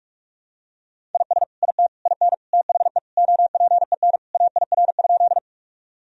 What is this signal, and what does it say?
Morse code, heard as a keyed tone, about 42 wpm, 700 Hz. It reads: IS IT IN THE OPEN AIR?
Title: Sentences from Top 1000 Words - Review Part 1 - 45wpm